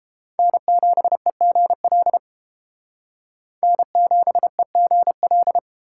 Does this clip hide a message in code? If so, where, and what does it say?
Yes: Morse code from 0.4 to 5.6 s, heard as a keyed tone, about 31 wpm, 700 Hz. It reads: N7EGL N7EGL